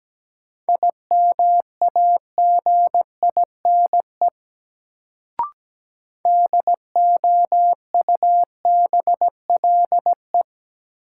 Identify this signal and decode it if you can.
Morse code, heard as a keyed tone, about 17 wpm, 700 Hz. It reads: IMAGINE DOUBLE
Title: Sentences from Top 1400 Words - New Word Drill Alt - 17wpm